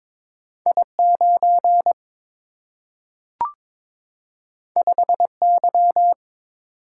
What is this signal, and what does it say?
Morse code, heard as a keyed tone, about 22 wpm, 700 Hz. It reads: I 9 5 Y
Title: Single-Letter Single-Number 22wpm